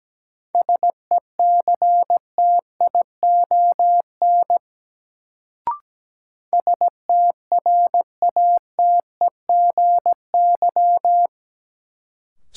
Morse code, heard as a keyed tone, about 17 wpm, 700 Hz. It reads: SECTION STRATEGY
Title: Sentences from Top 1000 Words - Review Part 5 - New Word Drill - 17wpm